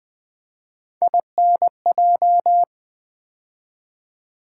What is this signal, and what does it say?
Morse code, heard as a keyed tone, about 20 wpm, 700 Hz. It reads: INJ
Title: Build A Word - 3 Letters At A Time - 9-Letter Words - 20wpm